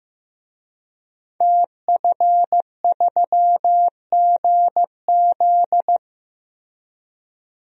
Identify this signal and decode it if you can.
Morse code, heard as a keyed tone, about 15 wpm, 700 Hz. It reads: TF3GZ
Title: DXCC-100 Call Signs - 5 Characters - Repeated 3x - 15wpm